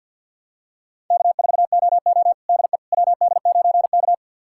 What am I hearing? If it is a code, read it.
Morse code, heard as a keyed tone, about 49 wpm, 700 Hz. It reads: K4OO DE WD9K
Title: USA Callsign DE Callsign - 4 Characters - 50wpm